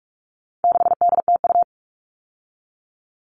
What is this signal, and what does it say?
Morse code, heard as a keyed tone, about 44 wpm, 700 Hz. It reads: 6DTU
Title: All Cut Numbers - 4-Digit 45wpm